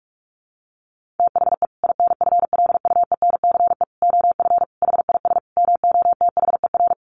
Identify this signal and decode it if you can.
Morse code, heard as a keyed tone, about 44 wpm, 700 Hz. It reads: THE INFLUENCE OF HIS MOTHER
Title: Sentences from Top 1100 Words 45wpm